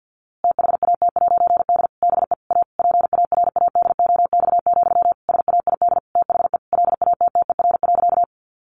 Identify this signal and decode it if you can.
Morse code, heard as a keyed tone, about 50 wpm, 700 Hz. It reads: THAT'D BE A PARADOX, SAID THE LATTER.
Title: A Sound Of Thunder 50wpm